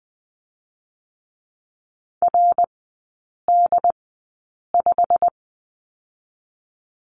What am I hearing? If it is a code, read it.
Morse code, heard as a keyed tone, about 20 wpm, spaced as for 8 wpm, 690 Hz. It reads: RD5
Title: 087 Lesson 15e - Number 5 - Three Characters 20wpm